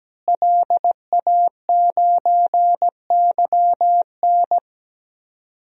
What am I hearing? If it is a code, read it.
Morse code, heard as a keyed tone, about 17 wpm, 700 Hz. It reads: LA9YN
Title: DXCC-100 Call Signs - 5 Characters - Repeated 2x - 17wpm